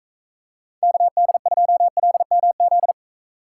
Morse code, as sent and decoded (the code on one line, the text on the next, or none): -.- -.. ..--- .-.. -- --..
KD2LMZ